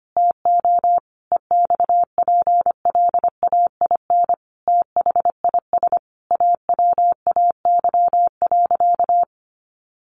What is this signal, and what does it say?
Morse code, heard as a keyed tone, about 25 wpm, 700 Hz. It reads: TO EXPLAIN THIS AWAY.
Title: Sentences from Top 1100 Words 25wpm